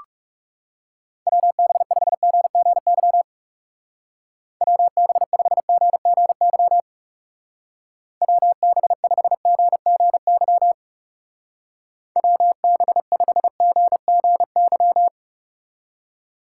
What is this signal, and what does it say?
Morse code, heard as a keyed tone, about 36 wpm, 700 Hz. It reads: WB5GGY WB5GGY WB5GGY WB5GGY
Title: Speed Racing - USA 2x3 Call Signs 30wpm